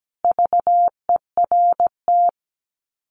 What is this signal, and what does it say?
Morse code, heard as a keyed tone, about 17 wpm, 700 Hz. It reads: VERT
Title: Common QSO Phrases 17wpm